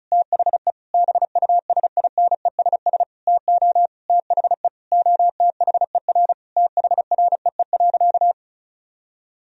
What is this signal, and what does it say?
Morse code, heard as a keyed tone, about 34 wpm, 700 Hz. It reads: THE BUSINESS TO THE OTHER THREE.
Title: Sentences from Top 1000 Words - Review Part 4 - 35wpm